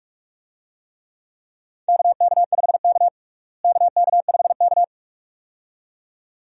Morse code, heard as a keyed tone, about 46 wpm, 700 Hz. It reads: KK5K KK5K
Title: USA Call Signs - 4 Characters - Repeated 2x - 45wpm